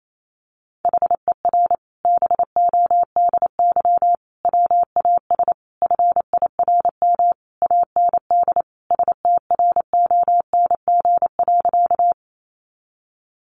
Morse code, heard as a keyed tone, about 28 wpm, 700 Hz. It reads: HER BODY WAS FIRM AND STRONG.
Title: Sentences from Top 1400 Words 28wpm